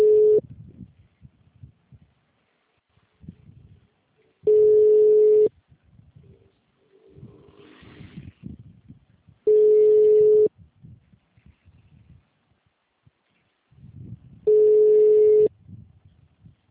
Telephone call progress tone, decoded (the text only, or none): ringback tone